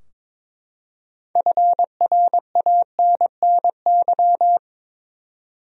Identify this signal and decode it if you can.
Morse code, heard as a keyed tone, about 23 wpm, 700 Hz. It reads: FRANNY
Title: USA Common Names - 22wpm